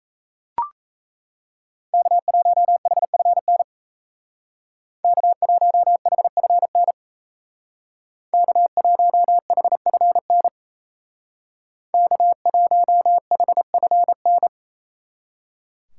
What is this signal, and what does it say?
Morse code, heard as a keyed tone, about 35 wpm, 700 Hz. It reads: K1HFN K1HFN K1HFN K1HFN
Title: Speed Racing - USA 1x3 Call Signs 28wpm